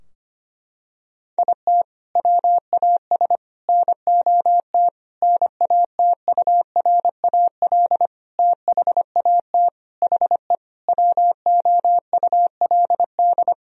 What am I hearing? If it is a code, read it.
Morse code, heard as a keyed tone, about 25 wpm, 700 Hz. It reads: IT WAS NOT NATURAL THAT HE WOULD
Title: Sentences from Top 1000 Words - Review Part 4 - 25wpm